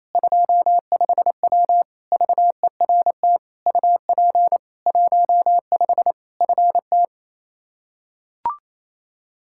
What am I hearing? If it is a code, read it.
Morse code, heard as a keyed tone, about 28 wpm, 700 Hz. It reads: 25W VERT UP 15 FT
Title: Practice QSO - Exchange 2 - Info - 28wpm